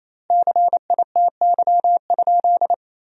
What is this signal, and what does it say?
Morse code, heard as a keyed tone, about 28 wpm, 700 Hz. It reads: CITY?
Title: 2x Word-Spacing - Sentences from Top 300 Words - 28wpm